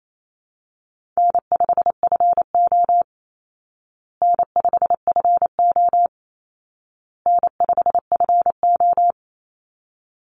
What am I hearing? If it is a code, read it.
Morse code, heard as a keyed tone, about 28 wpm, 700 Hz. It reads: N5FO N5FO N5FO
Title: Easy Call Signs Repeated 3x - 28wpm